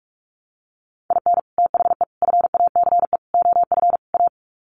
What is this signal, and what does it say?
Morse code, heard as a keyed tone, about 44 wpm, 700 Hz. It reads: IN THE FACE OF A
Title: Sets of 5 Words Encore 45wpm